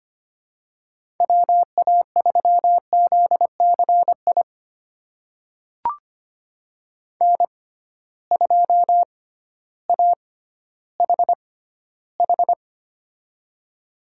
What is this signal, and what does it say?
Morse code, heard as a keyed tone, about 25 wpm, spaced as for 13 wpm, 700 Hz. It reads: WA3ZCI N2AHH
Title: USA Call Signs - 6x Character Spacing - 25wpm